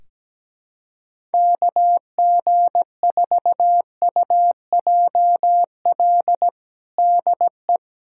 Morse code, heard as a keyed tone, about 17 wpm, 700 Hz. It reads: KG4UJL DE
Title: USA Callsign DE Callsign 17wpm